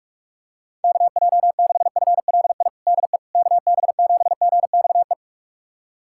Morse code, heard as a keyed tone, about 46 wpm, 700 Hz. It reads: KJ6FLI DE KB7GXE
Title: USA Callsign DE Callsign - 6 Characters - 45wpm